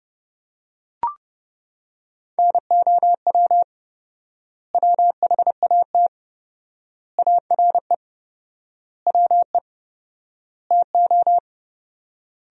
Morse code, heard as a keyed tone, about 32 wpm, 700 Hz. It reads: NOW WHAT ARE WE TO
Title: 4x Word-Spacing - Sentences from Top 100 Words - 30wpm